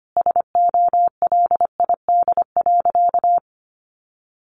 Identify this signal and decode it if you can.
Morse code, heard as a keyed tone, about 25 wpm, 700 Hz. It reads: SOLID.
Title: Sentences from Top 1500 Words 25wpm